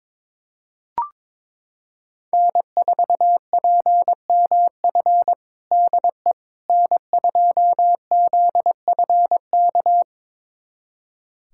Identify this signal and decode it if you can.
Morse code, heard as a keyed tone, about 21 wpm, 700 Hz. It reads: N4PMF DE N2ZFK
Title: USA Callsign DE Callsign - 5 Characters - 22wpm